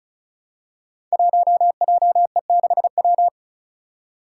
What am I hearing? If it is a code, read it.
Morse code, heard as a keyed tone, about 36 wpm, 700 Hz. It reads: 1JE6W
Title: Five Challenging Letter-Number Combinations 35wpm